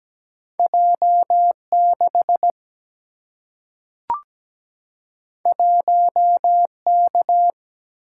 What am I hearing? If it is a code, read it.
Morse code, heard as a keyed tone, about 17 wpm, 700 Hz. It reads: J6 1K
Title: Single-Letter Single-Number 17wpm